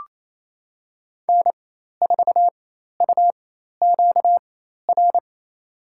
Morse code, heard as a keyed tone, about 28 wpm, spaced as for 13 wpm, 700 Hz. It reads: N4UQR
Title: USA Call Signs - 5 Characters - 4x Character Spacing - 28wpm